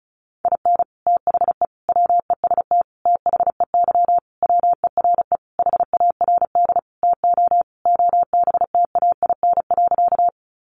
Morse code, heard as a keyed tone, about 34 wpm, 700 Hz. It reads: IN THE WEST THEY WERE HARD TO OBTAIN.
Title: Sentences from Top 1300 Words 35wpm